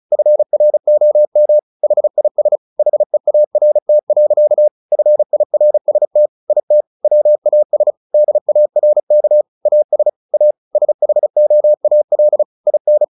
Morse code, heard as a keyed tone, about 34 wpm, 600 Hz. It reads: FROM HIS HEART. FIRST IT WAS DARK AS A SHOAL IN
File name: the old man and the sea 2 35wpm